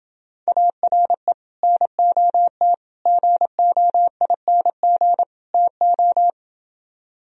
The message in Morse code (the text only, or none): ARE NOT GOING TO